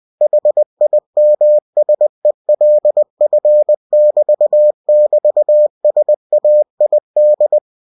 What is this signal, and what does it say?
Morse code, heard as a keyed tone, about 20 wpm, 600 Hz. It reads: HIMSELF<BT><BT>SAID
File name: cw-books / winnie-the-pooh / winnie-20wpm-page-0099